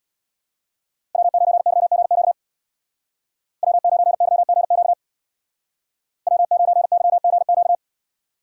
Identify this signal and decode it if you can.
Morse code, heard as a keyed tone, about 61 wpm, 700 Hz. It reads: W9YGQ W9YGQ W9YGQ
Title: Speed Racing - USA 1x3 Call Signs 50wpm